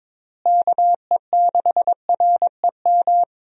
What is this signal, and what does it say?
Morse code, heard as a keyed tone, about 22 wpm, 700 Hz. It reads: KE6REM
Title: USA Call Signs - 6 Characters - Repeated 2x - 22wpm